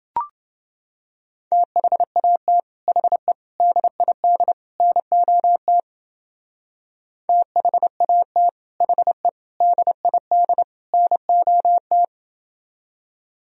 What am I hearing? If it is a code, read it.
Morse code, heard as a keyed tone, about 28 wpm, 700 Hz. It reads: THAT HE DID NOT THAT HE DID NOT